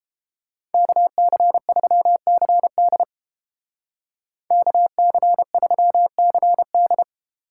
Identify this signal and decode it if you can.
Morse code, heard as a keyed tone, about 30 wpm, 700 Hz. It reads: KC3CD KC3CD